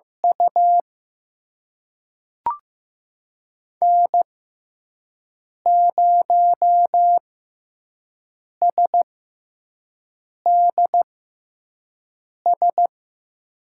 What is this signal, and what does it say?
Morse code, heard as a keyed tone, about 15 wpm, spaced as for 5 wpm, 700 Hz. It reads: U N0SDS
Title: USA Call Signs - 5 Characters - 6x Character Spacing - 15wpm